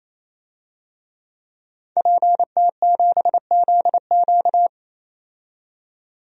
Morse code, heard as a keyed tone, about 28 wpm, 700 Hz. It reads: PT7ZQ
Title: DXCC-100 Call Signs - 5 Characters - 28wpm